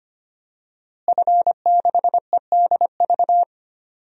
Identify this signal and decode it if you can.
Morse code, heard as a keyed tone, about 25 wpm, 700 Hz. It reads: F6EDV
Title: DXCC-100 Call Signs - 5 Characters - Repeated 2x - 25wpm